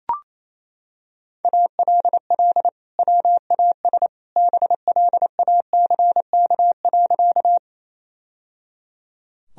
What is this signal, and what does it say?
Morse code, heard as a keyed tone, about 27 wpm, 700 Hz. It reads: ALL WAS BLACK.
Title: Sentences from Top 1000 Words - Review Part 2 - 28wpm